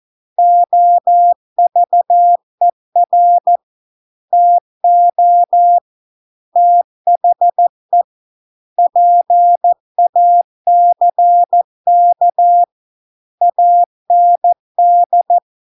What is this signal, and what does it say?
Morse code, heard as a keyed tone, about 14 wpm, 700 Hz. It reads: OVER TO THE PACK AND